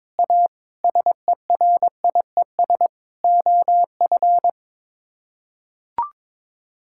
Morse code, heard as a keyed tone, about 21 wpm, 700 Hz. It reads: A SERIES OF
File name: practice / 22wpm / Sets of 8 Words - 22wpm